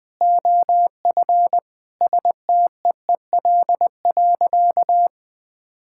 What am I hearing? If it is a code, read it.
Morse code, heard as a keyed tone, about 20 wpm, 700 Hz. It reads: OF STEEL.
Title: All-Age Appropriate Jokes - 20wpm